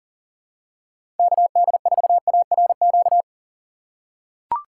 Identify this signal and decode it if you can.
Morse code, heard as a keyed tone, about 40 wpm, 700 Hz. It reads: KD4ARQ
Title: USA Call Signs - 6 Characters - 6x Character Spacing - 40wpm